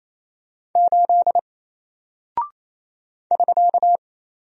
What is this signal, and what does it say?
Morse code, heard as a keyed tone, about 28 wpm, 700 Hz. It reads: 8 <SK>
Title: Single Character (Letter:Number:Punctuation:Prosign) Alt 28wpm